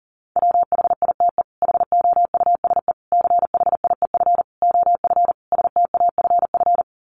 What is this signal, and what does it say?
Morse code, heard as a keyed tone, about 40 wpm, 700 Hz. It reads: WHITE HOUSE CHIEF OF STAFF
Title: Sets of 5 Words Encore 40wpm